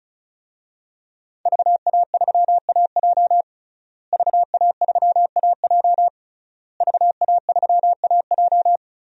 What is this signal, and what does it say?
Morse code, heard as a keyed tone, about 36 wpm, 700 Hz. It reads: VA3AJ VA3AJ VA3AJ
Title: DXCC-100 Call Signs - 5 Characters - Repeated 3x - 35wpm